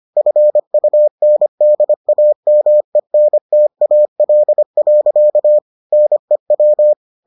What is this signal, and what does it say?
Morse code, heard as a keyed tone, about 25 wpm, 600 Hz. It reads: FUNDAMENTAL. NEW